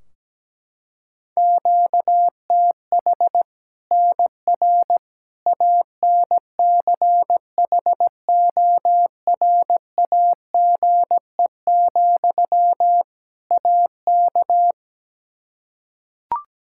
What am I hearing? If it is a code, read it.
Morse code, heard as a keyed tone, about 17 wpm, 700 Hz. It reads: QTH NR ANCHORAGE, AK
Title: K1USN SST QSO Practice 17wpm